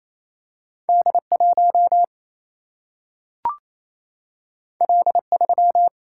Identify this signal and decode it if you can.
Morse code, heard as a keyed tone, about 28 wpm, 700 Hz. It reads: D 1 L 3
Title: Single-Letter Single-Number 28wpm